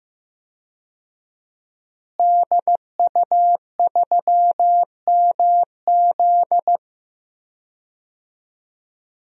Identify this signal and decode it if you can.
Morse code, heard as a keyed tone, about 15 wpm, 700 Hz. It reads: DU3MZ
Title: DXCC-100 Call Signs - 5 Characters - Repeated 2x - 15wpm